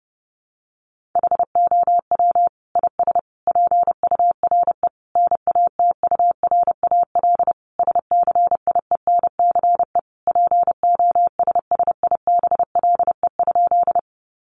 Morse code, heard as a keyed tone, about 30 wpm, 700 Hz. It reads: HOW IS PURE NATURAL SCIENCE POSSIBLE?
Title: Sentences from Top 1500 Words 30wpm